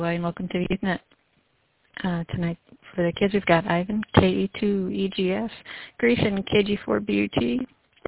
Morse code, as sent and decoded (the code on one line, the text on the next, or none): none